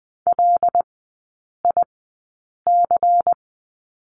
Morse code, heard as a keyed tone, about 20 wpm, spaced as for 8 wpm, 700 Hz. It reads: LIC
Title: 127 Lesson 21e - Number 6 - Three Characters 20wpm